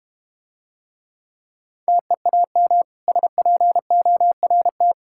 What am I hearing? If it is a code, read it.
Morse code, heard as a keyed tone, about 29 wpm, 690 Hz. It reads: TEAM SPORT
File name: practice / 25wpm / Difficult Words - Speed-Racing with Hints - 25wpm